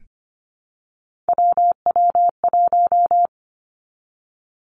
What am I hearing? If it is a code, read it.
Morse code, heard as a keyed tone, about 25 wpm, 700 Hz. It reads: WW1
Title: Three Challenging Letter-Number Combinations 25wpm